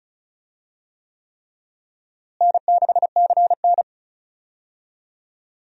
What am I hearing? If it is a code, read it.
Morse code, heard as a keyed tone, about 34 wpm, 700 Hz. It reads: N6CN
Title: USA Call Signs - 4 Characters - Repeated 3x - 35wpm